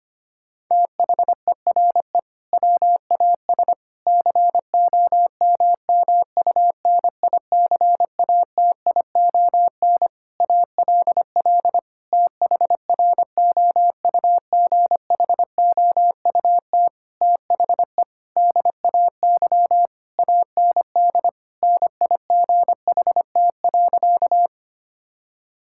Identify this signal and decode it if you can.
Morse code, heard as a keyed tone, about 25 wpm, 700 Hz. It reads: THERE WAS COMMUNICATION ALL THROUGHOUT THE DAY AND NIGHT.
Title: Sentences from Top 1000 Words - Review Part 6 - 25wpm